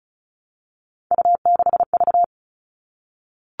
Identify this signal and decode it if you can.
Morse code, heard as a keyed tone, about 34 wpm, 700 Hz. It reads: U6V
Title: All Cut Numbers - 3-Digit 35wpm